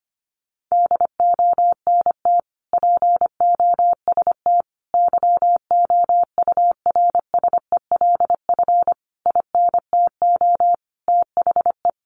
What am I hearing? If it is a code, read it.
Morse code, heard as a keyed tone, about 25 wpm, 700 Hz. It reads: DONT POST YOURSELF INTO THE
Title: Sentences from Top 1000 Words 25wpm